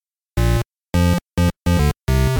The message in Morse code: none